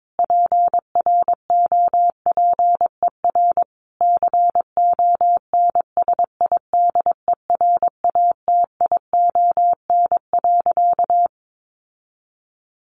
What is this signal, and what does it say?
Morse code, heard as a keyed tone, about 22 wpm, 700 Hz. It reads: PROPER CONSIDERATION.
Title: Sentences from Top 1500 Words 22wpm